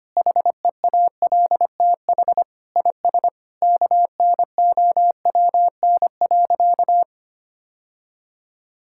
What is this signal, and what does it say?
Morse code, heard as a keyed tone, about 25 wpm, 700 Hz. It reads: HEALTH IS KNOWN.
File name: practice / 25wpm / Sentences with 3-Word Phrases - 25wpm